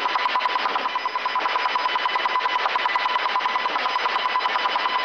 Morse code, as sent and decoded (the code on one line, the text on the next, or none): none